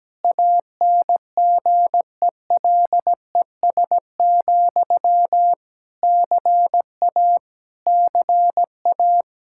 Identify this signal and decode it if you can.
Morse code, heard as a keyed tone, about 17 wpm, 700 Hz. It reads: ANGELES, CA CA
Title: Practice QSO - Exchange 1 - Beginning - 17wpm